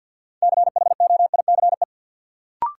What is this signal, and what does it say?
Morse code, heard as a keyed tone, about 53 wpm, 700 Hz. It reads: CHOICE
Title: Sets of 5 Words - New Word Drill - 50wpm